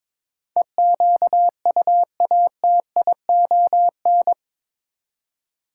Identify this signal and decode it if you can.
Morse code, heard as a keyed tone, about 22 wpm, 700 Hz. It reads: EQUATION
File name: practice / 22wpm / Eight Letter Words - 22wpm